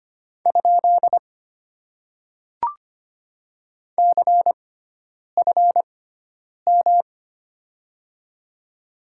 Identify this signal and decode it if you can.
Morse code, heard as a keyed tone, about 25 wpm, spaced as for 9 wpm, 700 Hz. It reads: ? CFM
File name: Top 90 QSO Elements - 6x Character Spacing - 25wpm